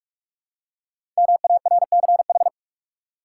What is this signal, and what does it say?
Morse code, heard as a keyed tone, about 45 wpm, 700 Hz. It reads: MARCH
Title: Five Letter Words - 45wpm